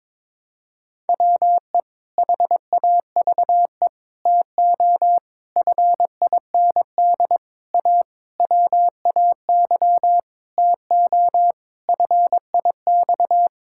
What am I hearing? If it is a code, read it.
Morse code, heard as a keyed tone, about 22 wpm, 700 Hz. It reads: WE HAVE TO FIND A WAY TO FIX
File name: Sentences with 7-Word Phrases - 22wpm